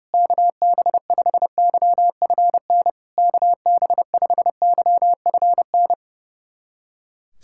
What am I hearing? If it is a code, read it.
Morse code, heard as a keyed tone, about 30 wpm, 690 Hz. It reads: KB5YFN KB5YFN
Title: SOTA QSO Practice 30wpm